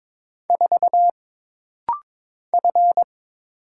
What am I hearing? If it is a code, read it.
Morse code, heard as a keyed tone, about 22 wpm, 700 Hz. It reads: 4 F
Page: This rapid-fire timing is geared to push your ICR (Instant Character Recognition) skills.